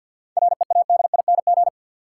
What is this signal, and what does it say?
Morse code, heard as a keyed tone, about 49 wpm, 700 Hz. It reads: READING
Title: Sentences from Top 600 Words - New Word Drill - 50wpm